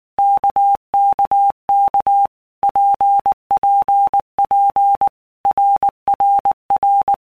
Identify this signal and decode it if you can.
Morse code, heard as a keyed tone, about 19 wpm, 800 Hz. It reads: KKK PPP RRR
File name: All Groups 1-minute Blocks / Groups@18 / RKPX